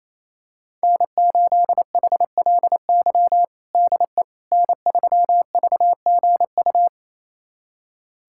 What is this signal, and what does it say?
Morse code, heard as a keyed tone, about 28 wpm, 700 Hz. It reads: N8HLY DE N3VGU
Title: USA Callsign DE Callsign - 5 Characters - 28wpm